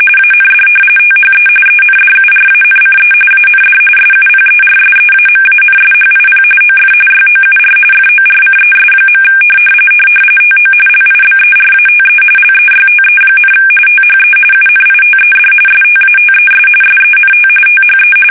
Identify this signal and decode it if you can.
RTTY, about 45.45 baud. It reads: AND THE ENDURANCE AND SUPERSEAMANSHIP OF YOUR FORCES. TO LIBERATE LEYTE, THE UNITED STATES SENT THE
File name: fsk850 / battleshipnj